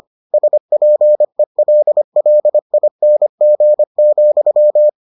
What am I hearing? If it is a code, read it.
Morse code, heard as a keyed tone, about 25 wpm, 600 Hz. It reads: SPELLING,